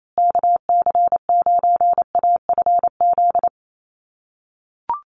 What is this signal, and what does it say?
Morse code, heard as a keyed tone, about 28 wpm, 700 Hz. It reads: KC9AFZ
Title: USA Call Signs Repeated 3x 28wpm